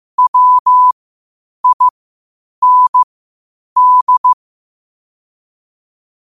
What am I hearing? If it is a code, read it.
Morse code, heard as a keyed tone, about 15 wpm, spaced as for 8 wpm, 1000 Hz. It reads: WIND
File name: qotdfiles / 8wpm